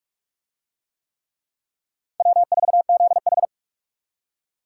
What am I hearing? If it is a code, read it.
Morse code, heard as a keyed tone, about 43 wpm, 700 Hz. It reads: W4ZH